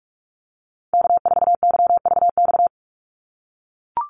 Morse code, heard as a keyed tone, about 44 wpm, 700 Hz. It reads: K4YVX
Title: USA Call Signs 45wpm